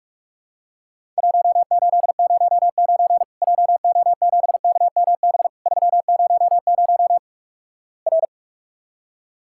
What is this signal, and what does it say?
Morse code, heard as a keyed tone, about 46 wpm, 700 Hz. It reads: 1809 JO7KMB 300 R
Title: Worked All Europe DX Contest - QTC Exchange - 45wpm